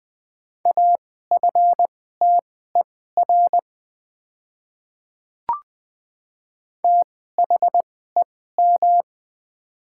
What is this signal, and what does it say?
Morse code, heard as a keyed tone, about 20 wpm, spaced as for 14 wpm, 700 Hz. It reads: AFTER THEM
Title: 2x Character Spacing - Top 300 Words - 20wpm